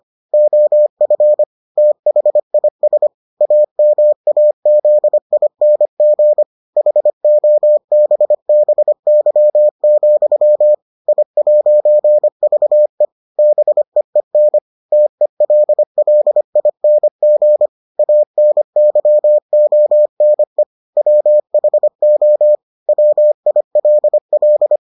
Morse code, heard as a keyed tone, about 25 wpm, 600 Hz. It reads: OF THIS AMAZING HOBBY, I'VE BEEN TELLING ANYONE WHO WILL